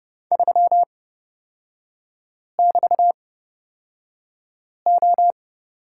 Morse code, heard as a keyed tone, about 30 wpm, spaced as for 5 wpm, 700 Hz. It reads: 3<BT>O
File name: Lesson 36e - BT - Three Characters 30wpm